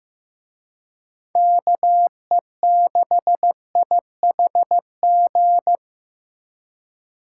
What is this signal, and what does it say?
Morse code, heard as a keyed tone, about 18 wpm, 700 Hz. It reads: KE6IHG
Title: Speed Racing - USA Call Signs 15wpm